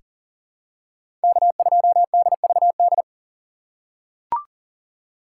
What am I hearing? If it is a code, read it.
Morse code, heard as a keyed tone, about 40 wpm, 700 Hz. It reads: K2DVD
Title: USA Call Signs - 5 Characters - 4x Character Spacing - 40wpm